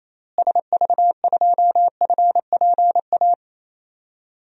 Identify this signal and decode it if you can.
Morse code, heard as a keyed tone, about 28 wpm, 700 Hz. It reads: SV2FPA
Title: DXCC-100 Call Signs - 6 Characters - Repeated 3x - 28wpm